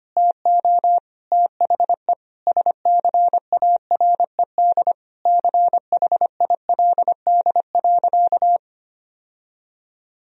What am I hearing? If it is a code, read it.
Morse code, heard as a keyed tone, about 25 wpm, 700 Hz. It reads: TO THE SCARED CHILD.
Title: Sentences from Top 1000 Words - Review Part 4 - 25wpm